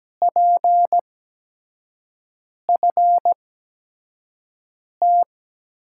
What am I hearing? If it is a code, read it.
Morse code, heard as a keyed tone, about 17 wpm, spaced as for 5 wpm, 700 Hz. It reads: PFT
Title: USA Call Signs - 5 Characters - 8x Character Spacing - 17wpm